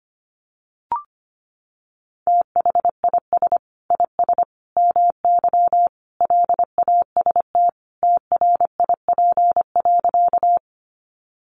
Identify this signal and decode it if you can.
Morse code, heard as a keyed tone, about 25 wpm, 700 Hz. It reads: THIS IS MY LAST TRIP.
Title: Sentences from Top 1300 Words 25wpm